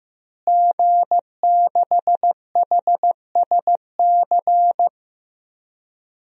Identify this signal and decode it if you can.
Morse code, heard as a keyed tone, about 15 wpm, 700 Hz. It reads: G6HSC
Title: DXCC-100 Call Signs - 5 Characters - Repeated 2x - 15wpm